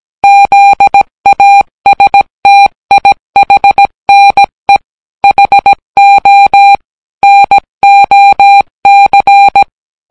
Morse code, heard as a keyed tone, about 17 wpm, 800 Hz. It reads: ZASTIHNE HO NOC